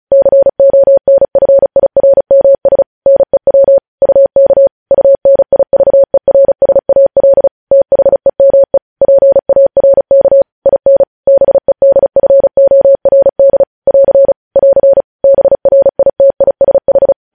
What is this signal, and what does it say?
Morse code, heard as a keyed tone, about 36 wpm, 550 Hz. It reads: CONFIRMS NEW UK UNIVERSAL THEME PARK IN BEDFORD <AR> <AR> BRITISH
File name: podcast / News headlines-1x-35x35